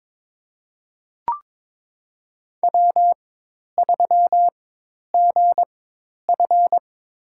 Morse code, heard as a keyed tone, about 22 wpm, spaced as for 10 wpm, 700 Hz. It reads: W3GF
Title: USA Call Signs - 4x Character Spacing - 22wpm